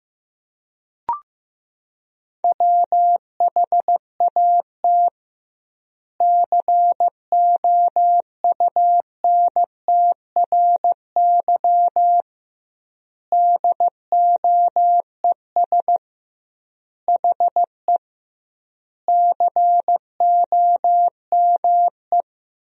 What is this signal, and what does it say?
Morse code, heard as a keyed tone, about 15 wpm, 700 Hz. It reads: WHAT COUNTRY DOES HE COME
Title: 2x Word-Spacing - Sentences from Top 300 Words - 15wpm